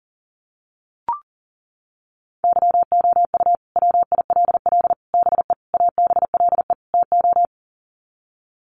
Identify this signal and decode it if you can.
Morse code, heard as a keyed tone, about 40 wpm, 700 Hz. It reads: YOU WILL BE ABLE TO
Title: Sets of 5 Words Encore2 40wpm